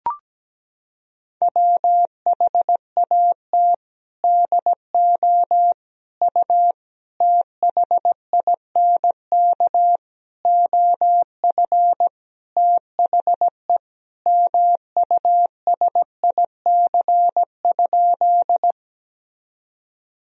Morse code, heard as a keyed tone, about 17 wpm, 700 Hz. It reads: WHAT DO U THINK OF THE MUSIC?
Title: Sentences from Top 300 Words and Abbreviations 17wpm